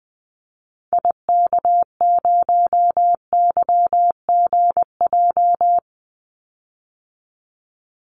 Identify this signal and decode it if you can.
Morse code, heard as a keyed tone, about 20 wpm, 700 Hz. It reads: IK0YGJ
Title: Call Signs Repeated 2x - 20wpm